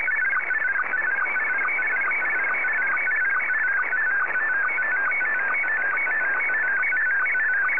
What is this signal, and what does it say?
SSTV (slow-scan television), Scottie 1.